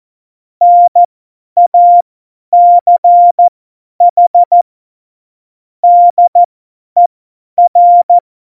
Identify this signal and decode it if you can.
Morse code, heard as a keyed tone, about 14 wpm, spaced as for 10 wpm, 700 Hz. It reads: NACH DER
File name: Morsekurs / Witze / Witz 20. August 2019, 50 F70